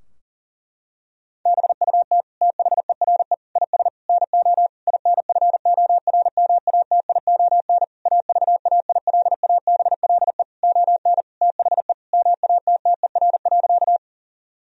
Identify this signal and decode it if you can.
Morse code, heard as a keyed tone, about 40 wpm, 700 Hz. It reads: BUT THERE IS NO INFORMATION AVAILABLE ON THE MATTER.
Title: Sentences from Top 1000 Words - Review Part 3 - 40wpm